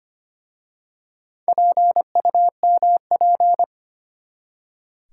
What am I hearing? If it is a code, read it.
Morse code, heard as a keyed tone, about 25 wpm, 700 Hz. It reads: PUMP
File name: practice / 25wpm / Easy Words with Hints - 25wpm